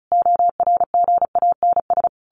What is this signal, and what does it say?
Morse code, heard as a keyed tone, about 34 wpm, 700 Hz. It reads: ORGANS
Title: A Sound Of Thunder 35wpm